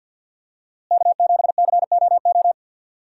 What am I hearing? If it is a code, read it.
Morse code, heard as a keyed tone, about 46 wpm, 700 Hz. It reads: K7COO
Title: USA Call Signs - 5 Characters - Repeated 3x - 50wpm